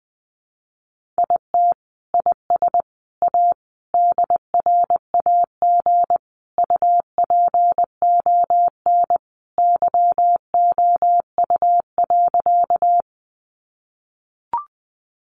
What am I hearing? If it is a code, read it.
Morse code, heard as a keyed tone, about 20 wpm, 700 Hz. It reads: IT IS A DRAG UPON YOU.
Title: Sentences from Top 1500 Words 20wpm